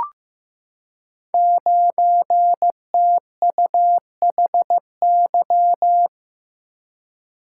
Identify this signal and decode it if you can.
Morse code, heard as a keyed tone, about 15 wpm, 700 Hz. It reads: 9TUHY
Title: Code Groups - 5 Letter:Numbers - 15wpm